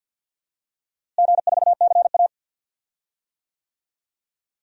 Morse code, heard as a keyed tone, about 46 wpm, 700 Hz. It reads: G4CA
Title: DXCC-100 Call Signs - 4 Characters - Repeated 2x - 50wpm